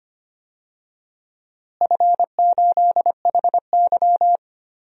Morse code, heard as a keyed tone, about 25 wpm, 700 Hz. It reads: F8HY
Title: DXCC-100 Call Signs - 4 Characters - Repeated 3x - 25wpm